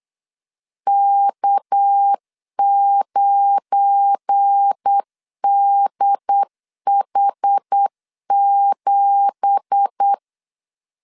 Morse code, heard as a keyed tone, about 8 wpm, 800 Hz. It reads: K9DH7
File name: CW / DET / 1 9 1500C S C 8wpm 5653